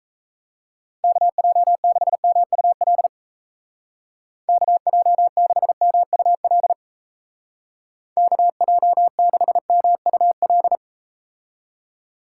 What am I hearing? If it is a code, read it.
Morse code, heard as a keyed tone, about 36 wpm, 700 Hz. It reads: KJ6MUL KJ6MUL KJ6MUL